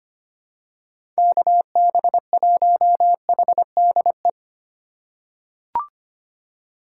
Morse code, heard as a keyed tone, about 25 wpm, 700 Hz. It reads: KB1HDE
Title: USA Call Signs - 4x Character Spacing - 25wpm